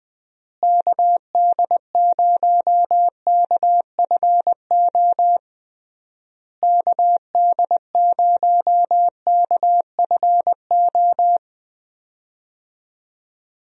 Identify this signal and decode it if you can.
Morse code, heard as a keyed tone, about 20 wpm, 700 Hz. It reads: KD0KFO KD0KFO
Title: USA Call Signs - 6 Characters - Repeated 3x - 20wpm